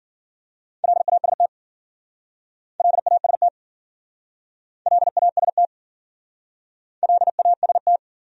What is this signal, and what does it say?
Morse code, heard as a keyed tone, about 51 wpm, 700 Hz. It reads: LAST LAST LAST LAST